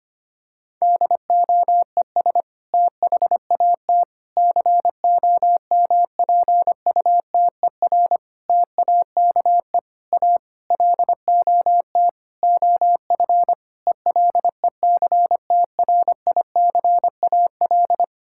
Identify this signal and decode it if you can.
Morse code, heard as a keyed tone, about 25 wpm, 700 Hz. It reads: DOES THAT COMPUTER TAKE A LOT OF ELECTRICAL
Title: Sentences from Top 1000 Words - Review Part 6 - 25wpm